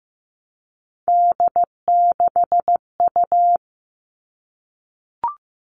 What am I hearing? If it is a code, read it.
Morse code, heard as a keyed tone, about 15 wpm, 700 Hz. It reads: D6U
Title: All Cut Numbers - 3-Digit 15wpm